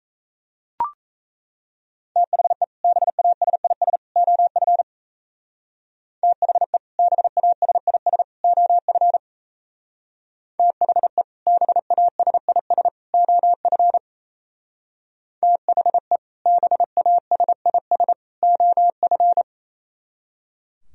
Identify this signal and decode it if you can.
Morse code, heard as a keyed tone, about 36 wpm, 700 Hz. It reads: THE BASIS OF THE BASIS OF THE BASIS OF THE BASIS OF